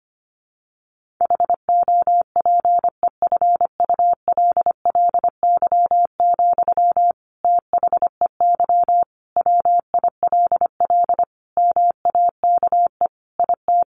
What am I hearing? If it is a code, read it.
Morse code, heard as a keyed tone, about 25 wpm, 700 Hz. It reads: HOPEFULLY, THEY WILL MAKE IT
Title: Sentences from Top 1500 Words 25wpm